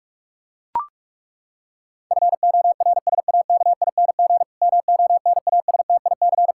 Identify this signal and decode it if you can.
Morse code, heard as a keyed tone, about 41 wpm, 700 Hz. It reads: FORSAKING MONASTIC
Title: Pangram Challenge v2 - 45wpm